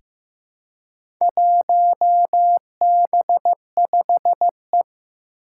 Morse code, heard as a keyed tone, about 15 wpm, 700 Hz. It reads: 1B5E
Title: Four Challenging Letter-Number Combinations 15wpm